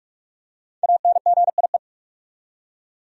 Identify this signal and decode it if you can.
Morse code, heard as a keyed tone, about 46 wpm, 700 Hz. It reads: ANGIE
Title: USA Common Names - 45wpm